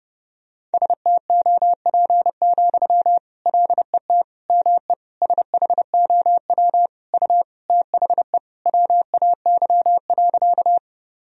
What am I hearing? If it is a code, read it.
Morse code, heard as a keyed tone, about 30 wpm, 700 Hz. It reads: STOP, LET ME SHOW U THE WAY.
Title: Sentences from Top 300 Words and Abbreviations 30wpm